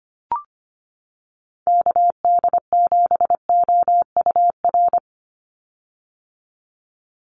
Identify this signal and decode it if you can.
Morse code, heard as a keyed tone, about 25 wpm, 690 Hz. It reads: KD7OUR
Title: USA Call Signs 25wpm